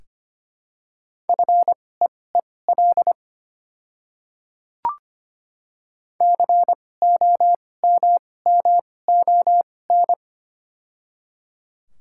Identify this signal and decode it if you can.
Morse code, heard as a keyed tone, about 25 wpm, spaced as for 18 wpm, 700 Hz. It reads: FEEL COMMON
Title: 2x Character Spacing - Top 500 Words - 25wpm